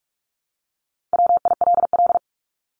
Morse code, heard as a keyed tone, about 44 wpm, 700 Hz. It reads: WILL
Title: Top 300 Words 45wpm